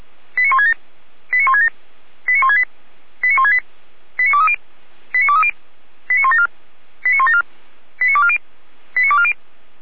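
Five-tone selective call (ZVEI), heard as quick runs of five tones, tones of about 70 ms. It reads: 89167 89167 89168 89168 89230 89230 89174 89174 89240 89240
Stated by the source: ZVEI five-tone selective call 89164, received by SDR.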